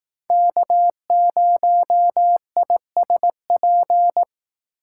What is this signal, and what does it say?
Morse code, heard as a keyed tone, about 18 wpm, 700 Hz. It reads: K0ISP
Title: Speed Racing - USA 1x3 Call Signs 15wpm